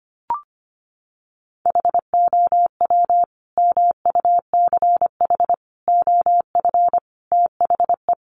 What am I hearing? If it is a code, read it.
Morse code, heard as a keyed tone, about 25 wpm, 700 Hz. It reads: HOW MUCH OF THE
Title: Sentences from Top 1200 Words 25wpm